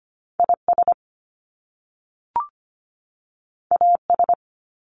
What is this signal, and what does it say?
Morse code, heard as a keyed tone, about 25 wpm, 700 Hz. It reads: IS AS
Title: Top Two Letter Combinations in Words 25wpm